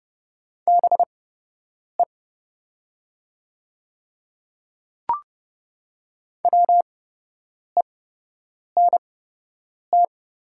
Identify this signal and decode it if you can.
Morse code, heard as a keyed tone, about 30 wpm, spaced as for 8 wpm, 700 Hz. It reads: BE WENT
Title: 8x Character Spacing - Top 300 Words - 30wpm